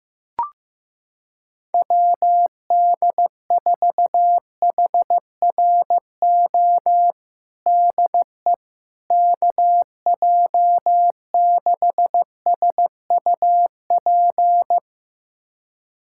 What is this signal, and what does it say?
Morse code, heard as a keyed tone, about 15 wpm, 700 Hz. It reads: WD4HRO DE KJ6SUP
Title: USA Callsign DE Callsign 15wpm